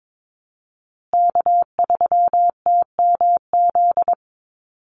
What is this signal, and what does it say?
Morse code, heard as a keyed tone, about 22 wpm, 700 Hz. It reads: K3TMZ
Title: USA Call Signs Repeated 2x 22wpm